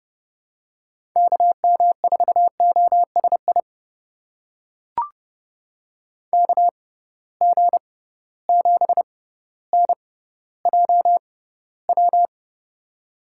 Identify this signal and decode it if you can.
Morse code, heard as a keyed tone, about 30 wpm, spaced as for 15 wpm, 700 Hz. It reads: KM4OSI KG7NJW
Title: USA Call Signs - 6 Characters - 6x Character Spacing - 30wpm